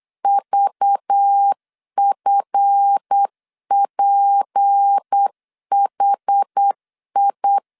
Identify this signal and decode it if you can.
Morse code, heard as a keyed tone, about 8 wpm, 800 Hz. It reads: VFPHI
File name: LEZIONI / 7 GF9Z / 7 GF 8wpm 3108